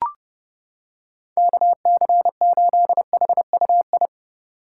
Morse code, heard as a keyed tone, about 31 wpm, 700 Hz. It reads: KC8HUI